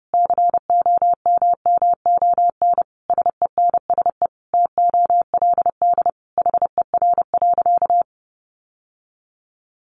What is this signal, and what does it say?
Morse code, heard as a keyed tone, about 30 wpm, 700 Hz. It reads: COMMON SENSE TOLD HER.
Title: Sentences from Top 1200 Words 30wpm